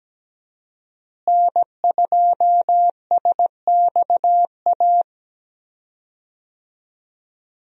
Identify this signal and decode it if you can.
Morse code, heard as a keyed tone, about 17 wpm, 700 Hz. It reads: N2SXA
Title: USA Call Signs - 5 Characters - Repeated 2x - 17wpm